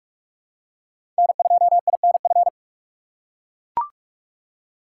Morse code, heard as a keyed tone, about 42 wpm, 700 Hz. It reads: N2INF
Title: USA Call Signs - 5 Characters - 6x Character Spacing - 45wpm